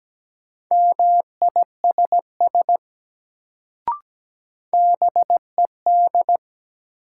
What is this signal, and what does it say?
Morse code, heard as a keyed tone, about 17 wpm, 700 Hz. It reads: MISS BED
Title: Sentences from Top 500 Words - New Word Drill - 17wpm